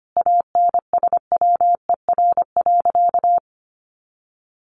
Morse code, heard as a keyed tone, about 25 wpm, 700 Hz. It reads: ANSWER.
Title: Sentences from Top 1100 Words 25wpm